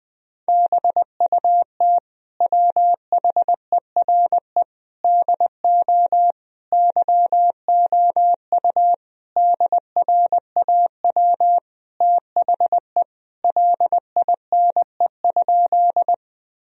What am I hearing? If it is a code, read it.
Morse code, heard as a keyed tone, about 20 wpm, 700 Hz. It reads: BUT WHERE DO YOU DRAW THE LINE?
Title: Sentences from Top 1000 Words - Review Part 1 - 20wpm